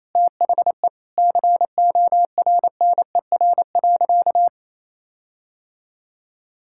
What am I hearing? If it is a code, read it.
Morse code, heard as a keyed tone, about 28 wpm, 700 Hz. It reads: THE CORNER.
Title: Sentences from Top 1600 Words 28wpm